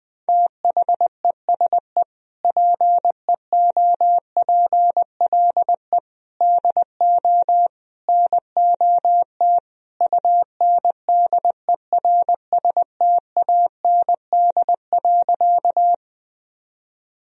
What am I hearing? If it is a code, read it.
Morse code, heard as a keyed tone, about 20 wpm, 700 Hz. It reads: THESE PEOPLE DO NOT UNDERSTAND.
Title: Sentences from Top 1000 Words - Review Part 1 - 20wpm